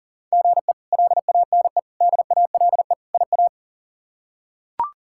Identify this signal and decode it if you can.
Morse code, heard as a keyed tone, about 40 wpm, 700 Hz. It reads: GE LANE DALE IA
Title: K1USN SST QSO Practice - 40wpm